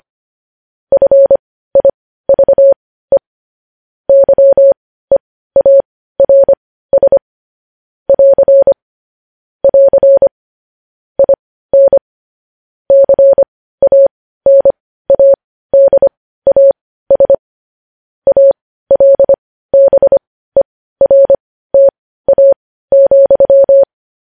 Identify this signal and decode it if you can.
Morse code, heard as a keyed tone, about 25 wpm, spaced as for 15 wpm, 550 Hz. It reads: FIVE YEARS <AR> <AR> IN CANADAS ALBERTA,
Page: News headlines at 15